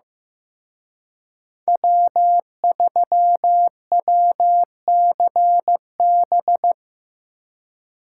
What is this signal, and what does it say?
Morse code, heard as a keyed tone, about 18 wpm, 700 Hz. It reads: W3WCB
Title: Speed Racing - USA 1x3 Call Signs 15wpm